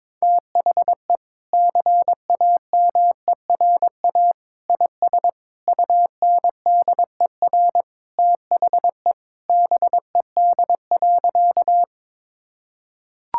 Morse code, heard as a keyed tone, about 22 wpm, 700 Hz. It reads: THE CAMERA IS UNDER THE BED.
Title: Sentences from Top 1000 Words - Review Part 5 - 22wpm